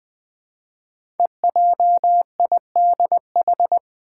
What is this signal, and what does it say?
Morse code, heard as a keyed tone, about 20 wpm, 700 Hz. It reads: EJIDH
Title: Code Groups - 5 Letters - 20wpm